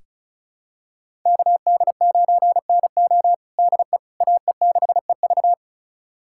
Morse code, heard as a keyed tone, about 36 wpm, 700 Hz. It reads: KD9NO DE AE6EV
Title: USA Callsign DE Callsign - 5 Characters - 35wpm